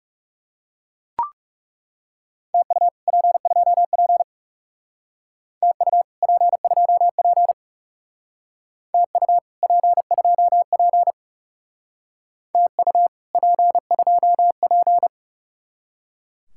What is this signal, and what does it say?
Morse code, heard as a keyed tone, about 38 wpm, 700 Hz. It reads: TU P2P TU P2P TU P2P TU P2P